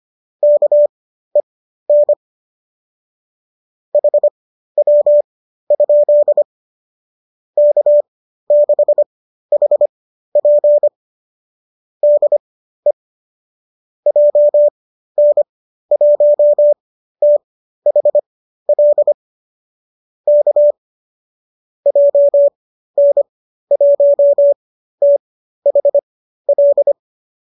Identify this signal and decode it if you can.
Morse code, heard as a keyed tone, about 25 wpm, spaced as for 13 wpm, 590 Hz. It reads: KEN HW? K6HP DE JN1THL K JN1THL